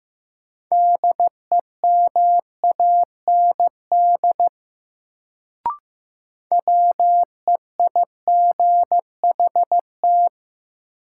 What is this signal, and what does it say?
Morse code, heard as a keyed tone, about 15 wpm, 700 Hz. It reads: DEMAND WEIGHT
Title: Sentences from Top 800 Words - New Word Drill - 15wpm